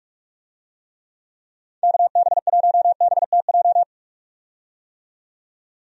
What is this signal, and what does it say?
Morse code, heard as a keyed tone, about 46 wpm, 700 Hz. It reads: KB1BTJ
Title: USA Call Signs - 6 Characters - 45wpm